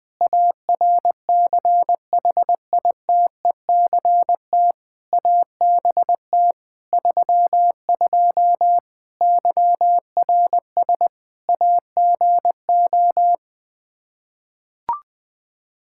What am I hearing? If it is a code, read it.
Morse code, heard as a keyed tone, about 21 wpm, 700 Hz. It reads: ARCHITECT ABT 32 YRS AGO
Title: Practice QSO - Exchange 3 - More Info - 20wpm